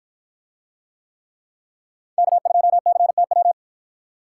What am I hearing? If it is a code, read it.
Morse code, heard as a keyed tone, about 53 wpm, 700 Hz. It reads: K2CTW